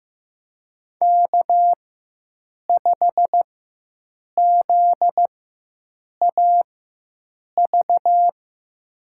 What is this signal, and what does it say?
Morse code, heard as a keyed tone, about 15 wpm, spaced as for 7 wpm, 700 Hz. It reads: K5ZAV
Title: USA Call Signs - 4x Character Spacing - 15wpm